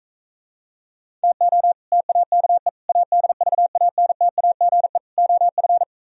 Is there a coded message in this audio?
Yes: Morse code from 1.2 to 5.8 s, heard as a keyed tone, about 45 wpm, 690 Hz. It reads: TO TAKE ADVANTAGE OF